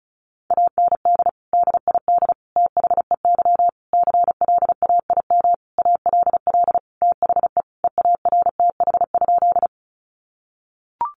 Morse code, heard as a keyed tone, about 34 wpm, 700 Hz. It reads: AND DID THEY CLAIM ALL THE EARTH?
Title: Sentences from Top 1500 Words 35wpm